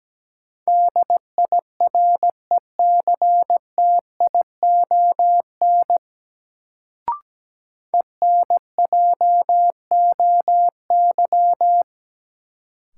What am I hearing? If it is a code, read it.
Morse code, heard as a keyed tone, about 17 wpm, 700 Hz. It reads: DIRECTION ENJOY
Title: Sentences from Top 1300 Words - New Word Drill Alt - 17wpm